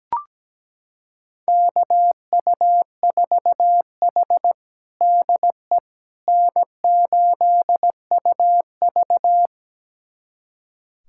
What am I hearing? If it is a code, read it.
Morse code, heard as a keyed tone, about 17 wpm, 700 Hz. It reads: KU4H DE N8UV
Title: USA Callsign DE Callsign - 4 Characters - 17wpm